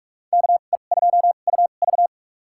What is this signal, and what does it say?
Morse code, heard as a keyed tone, about 46 wpm, spaced as for 32 wpm, 700 Hz. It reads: KE2UV
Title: USA Call Signs - 5 Characters - 2x Character Spacing - 45wpm